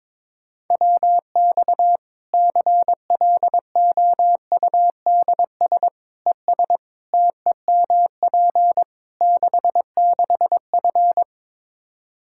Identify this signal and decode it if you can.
Morse code, heard as a keyed tone, about 22 wpm, 700 Hz. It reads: WX CLOUDS ES TEMP 66F
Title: Practice QSO - Exchange 2 - Info - 22wpm